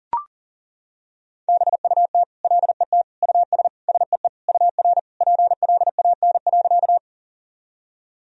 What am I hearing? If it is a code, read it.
Morse code, heard as a keyed tone, about 40 wpm, 700 Hz. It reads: BUT LET US SEE UR PLAN.
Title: Sentences from Top 500 Words and Abbreviations 40wpm